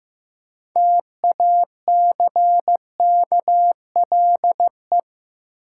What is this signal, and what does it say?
Morse code, heard as a keyed tone, about 15 wpm, 700 Hz. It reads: TACKLE
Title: Six Letter Words - 15wpm